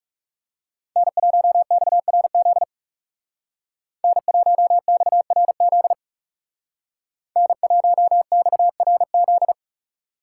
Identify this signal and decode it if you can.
Morse code, heard as a keyed tone, about 37 wpm, 700 Hz. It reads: N1XRZ N1XRZ N1XRZ
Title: Speed Racing - USA 1x3 Call Signs 30wpm